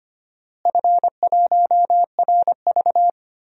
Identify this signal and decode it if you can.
Morse code, heard as a keyed tone, about 25 wpm, 700 Hz. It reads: F1RV
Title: DXCC-100 Call Signs - 4 Characters - Repeated 3x - 25wpm